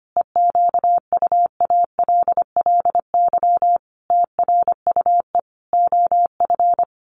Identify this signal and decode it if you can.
Morse code, heard as a keyed tone, about 25 wpm, 700 Hz. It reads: EQUALLY TRUE OF
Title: Sentences from Top 1400 Words 25wpm